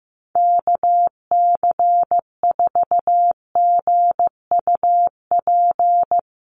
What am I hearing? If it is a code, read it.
Morse code, heard as a keyed tone, about 14 wpm, 700 Hz. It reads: KC4GUP
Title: USA Call Signs 15wpm